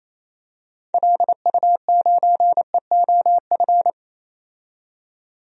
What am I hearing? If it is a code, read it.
Morse code, heard as a keyed tone, about 28 wpm, 700 Hz. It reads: LU9EOF
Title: DXCC-100 Call Signs - 6 Characters - 28wpm